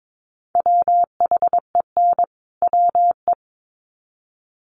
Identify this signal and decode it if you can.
Morse code, heard as a keyed tone, about 22 wpm, 700 Hz. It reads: WHEN WE
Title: Sets of 2 Words 22wpm